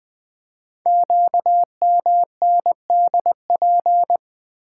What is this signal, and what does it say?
Morse code, heard as a keyed tone, about 20 wpm, 700 Hz. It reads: QMNDP
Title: Code Groups - 5 Letters - 20wpm